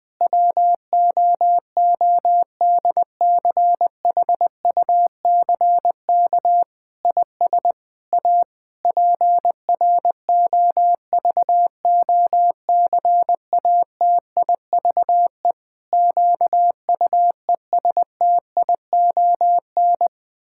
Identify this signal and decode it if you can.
Morse code, heard as a keyed tone, about 20 wpm, 700 Hz. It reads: WOODCHUCK IS A PROVOCATIVE QUESTION
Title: Pangram Challenge v2 - 20wpm